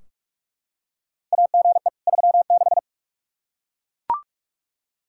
Morse code, heard as a keyed tone, about 41 wpm, 700 Hz. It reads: AGE 36
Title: Practice QSO - Exchange 3 - More Info - 45wpm